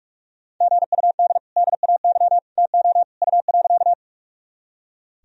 Morse code, heard as a keyed tone, about 41 wpm, 700 Hz. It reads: GUD DAY TO U.
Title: Sentences from Top 200 Words and Abbreviations 45wpm